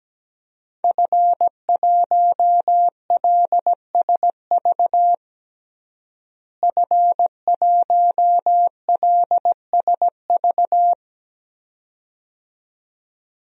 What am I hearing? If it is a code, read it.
Morse code, heard as a keyed tone, about 17 wpm, 700 Hz. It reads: F1LSV F1LSV
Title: DXCC-100 Call Signs - 5 Characters - Repeated 2x - 17wpm